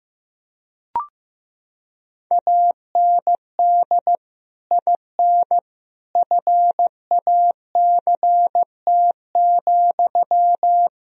Morse code, heard as a keyed tone, about 16 wpm, 700 Hz. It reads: AND IN FACT,
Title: Sentences from Top 1000 Words - Review Part 3 - 15wpm